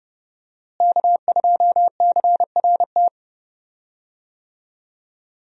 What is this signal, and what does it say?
Morse code, heard as a keyed tone, about 30 wpm, 700 Hz. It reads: K2CRT
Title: USA Call Signs - 5 Characters - Repeated 3x - 30wpm